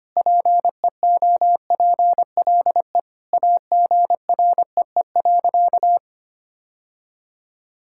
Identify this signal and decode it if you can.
Morse code, heard as a keyed tone, about 25 wpm, 700 Hz. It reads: PEOPLE AGREE.
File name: practice / 25wpm / Sentences with 4-Word Phrases - 25wpm